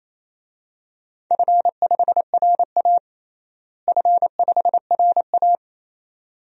Morse code, heard as a keyed tone, about 28 wpm, 700 Hz. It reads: F5RA F5RA
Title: DXCC-100 Call Signs - 4 Characters - Repeated 2x - 28wpm